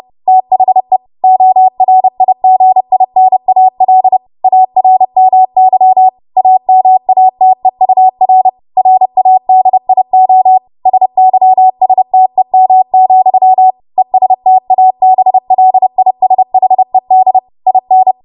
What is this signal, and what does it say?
Morse code, heard as a keyed tone, about 30 wpm, 750 Hz. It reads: THE ORIGINAL ARMY AMATEUR RADIO SYSTEM, ESTABLISHED IN
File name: Archive / 30 WPM / 30WPM